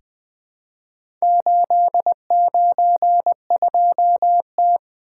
Morse code, heard as a keyed tone, about 20 wpm, 700 Hz. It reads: 892T
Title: Four Letter-Number Combinations - 20wpm